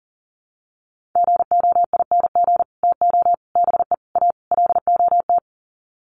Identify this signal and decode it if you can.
Morse code, heard as a keyed tone, about 40 wpm, 700 Hz. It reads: GOING TO BE A LOT
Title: Sets of 5 Words Encore 40wpm